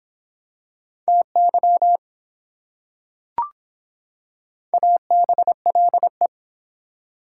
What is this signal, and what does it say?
Morse code, heard as a keyed tone, about 24 wpm, 700 Hz. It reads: TY ABLE